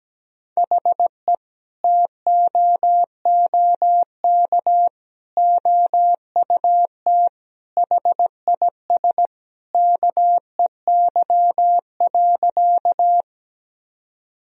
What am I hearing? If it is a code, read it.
Morse code, heard as a keyed tone, about 17 wpm, 700 Hz. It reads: HE TOOK OUT HIS KEY.
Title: Sentences from Top 1000 Words - Review Part 4 - 17wpm